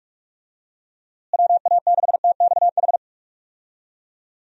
Morse code, heard as a keyed tone, about 46 wpm, 700 Hz. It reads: WA6TXH
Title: USA Call Signs - 6 Characters - Repeated 2x - 45wpm